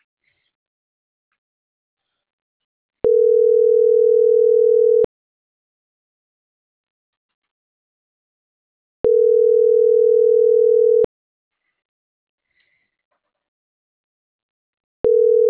Ringback tone (North America), 440 + 480 Hz.